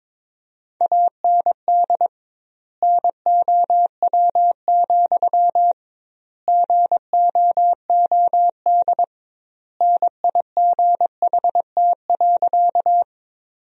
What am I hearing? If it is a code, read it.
Morse code, heard as a keyed tone, about 21 wpm, 700 Hz. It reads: AND NOW, GOOD NIGHT.
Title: 2x Word-Spacing - Sentences from Top 300 Words - 22wpm